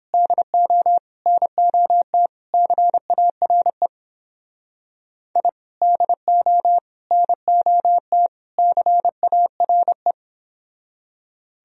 Morse code, heard as a keyed tone, about 31 wpm, 700 Hz. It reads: DO NOT CARE I DO NOT CARE